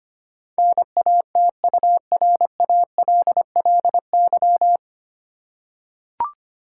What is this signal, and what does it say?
Morse code, heard as a keyed tone, about 25 wpm, 700 Hz. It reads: NATURALLY
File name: practice / 25wpm / 9 Letter Words - First 3 Letters Given - 25wpm